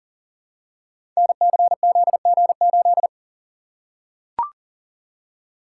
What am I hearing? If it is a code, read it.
Morse code, heard as a keyed tone, about 40 wpm, 700 Hz. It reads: NCZG8
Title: Code Groups - 5 Letter:Numbers - 40wpm